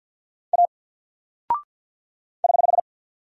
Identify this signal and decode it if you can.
Morse code, heard as a keyed tone, about 50 wpm, 700 Hz. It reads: A <HH>
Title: Single Character (Letter:Number:Punctuation:Prosign) 50wpm